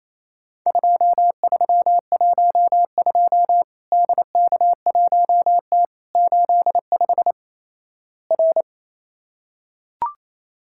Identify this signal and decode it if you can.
Morse code, heard as a keyed tone, about 28 wpm, 700 Hz. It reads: 2312 DK1T 85 R
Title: Worked All Europe DX Contest - QTC Exchange - 28wpm